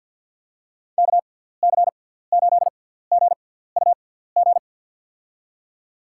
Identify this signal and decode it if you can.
Morse code, heard as a keyed tone, about 53 wpm, spaced as for 17 wpm, 700 Hz. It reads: KC8GUG
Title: USA Call Signs - 6 Characters - 6x Character Spacing - 50wpm